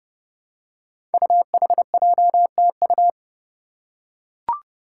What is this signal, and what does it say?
Morse code, heard as a keyed tone, about 30 wpm, 700 Hz. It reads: UHJTU
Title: Five Challenging Letter-Number Combinations 30wpm